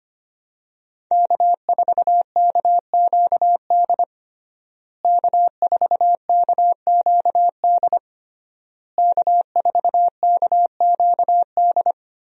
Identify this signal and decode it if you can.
Morse code, heard as a keyed tone, about 25 wpm, 700 Hz. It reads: K4KQD K4KQD K4KQD
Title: USA Call Signs - 5 Characters - Repeated 3x - 25wpm